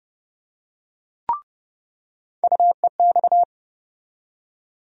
Morse code, heard as a keyed tone, about 31 wpm, 700 Hz. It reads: UEX
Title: USA Call Sign Postfixes - 3 Letters 30wpm